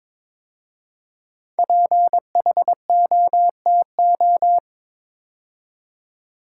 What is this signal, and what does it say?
Morse code, heard as a keyed tone, about 22 wpm, 700 Hz. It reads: PHOTO
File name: 5 Letter Words - First 3 Letters Given - 22wpm